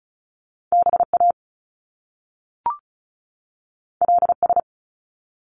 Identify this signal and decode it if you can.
Morse code, heard as a keyed tone, about 36 wpm, 700 Hz. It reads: BA LS
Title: Top Two Letter Combinations in Words 35wpm